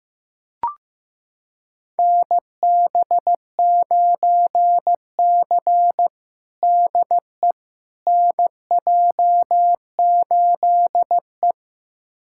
Morse code, heard as a keyed tone, about 15 wpm, 700 Hz. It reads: NB9C DE NJ8E
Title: USA Callsign DE Callsign - 4 Characters - 15wpm